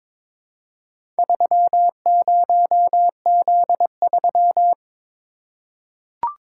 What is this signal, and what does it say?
Morse code, heard as a keyed tone, about 22 wpm, 700 Hz. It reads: 30Z3
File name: practice / 22wpm / Four Letter-Number Combinations - 22wpm